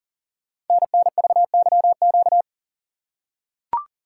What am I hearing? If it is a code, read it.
Morse code, heard as a keyed tone, about 44 wpm, 700 Hz. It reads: NNVYQ
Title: Code Groups - 5 Letters - 40wpm